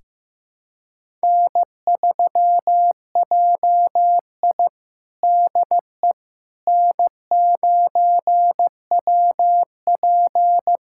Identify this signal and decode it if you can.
Morse code, heard as a keyed tone, about 16 wpm, 700 Hz. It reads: N3JI DE N9WP
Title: USA Callsign DE Callsign - 4 Characters - 15wpm